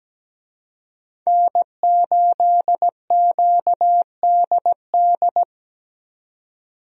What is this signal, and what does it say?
Morse code, heard as a keyed tone, about 17 wpm, 700 Hz. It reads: N8QDD
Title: USA Call Signs - 5 Characters - 17wpm